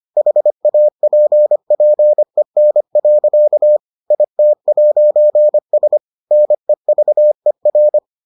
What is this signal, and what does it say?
Morse code, heard as a keyed tone, about 25 wpm, 600 Hz. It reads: HAPPEN. IT'S NEVER